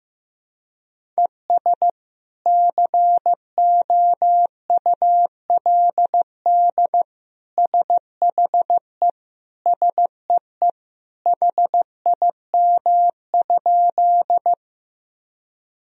Morse code, heard as a keyed tone, about 15 wpm, 700 Hz. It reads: ES COULD SHE SEE HIM?
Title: Sentences from Top 100 Words and Abbreviations 15wpm